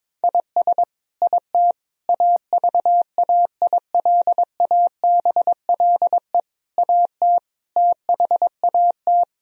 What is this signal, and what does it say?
Morse code, heard as a keyed tone, about 22 wpm, 700 Hz. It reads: IS IT AVAILABLE AT THAT
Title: Sentences from Top 1000 Words - Review Part 3 - 22wpm